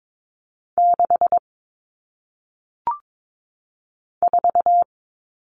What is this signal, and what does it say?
Morse code, heard as a keyed tone, about 22 wpm, 700 Hz. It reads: 6 4
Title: Numbers - 1-Digit 22wpm